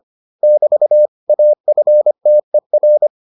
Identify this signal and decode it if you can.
Morse code, heard as a keyed tone, about 25 wpm, 600 Hz. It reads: <BT> AFTER